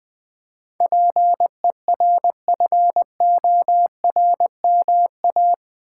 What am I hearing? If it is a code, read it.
Morse code, heard as a keyed tone, about 20 wpm, 700 Hz. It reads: PERFORMA
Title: Build A Word - 4 Letters At A Time - 11-Letter Words - 20wpm